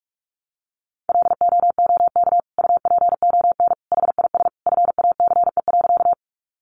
Morse code, heard as a keyed tone, about 44 wpm, 700 Hz. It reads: LOOK UPON HIS FACE.
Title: Sentences from Top 1100 Words 45wpm